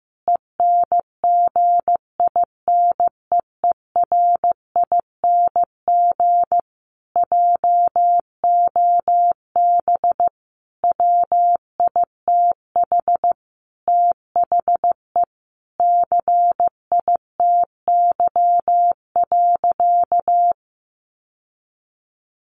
Morse code, heard as a keyed tone, about 15 wpm, 700 Hz. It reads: ENGINEERING JOB WITH THE CITY.
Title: Sentences from Top 1500 Words 15wpm